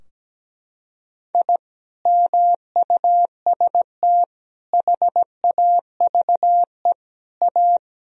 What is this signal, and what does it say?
Morse code, heard as a keyed tone, about 17 wpm, 700 Hz. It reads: I MUST HAVE A
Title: Sentences from Top 1000 Words - Review Part 4 - 17wpm